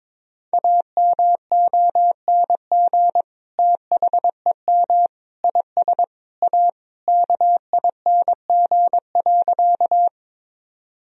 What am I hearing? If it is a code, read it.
Morse code, heard as a keyed tone, about 22 wpm, 700 Hz. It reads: AMONG THEM IS A KING.
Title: Sentences from Top 1000 Words - Review Part 3 - 22wpm